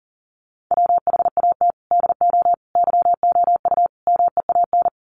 Morse code, heard as a keyed tone, about 40 wpm, 700 Hz. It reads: WHAT DO YOU MEAN
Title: Sets of 4 Words 40wpm